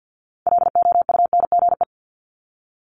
Morse code, heard as a keyed tone, about 50 wpm, 690 Hz. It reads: LOUNGE
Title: Multi-letter Phonemes 50wpm